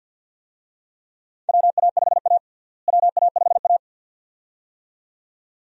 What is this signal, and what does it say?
Morse code, heard as a keyed tone, about 52 wpm, 700 Hz. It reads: WA5A WA5A
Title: USA Call Signs - 4 Characters - Repeated 2x - 50wpm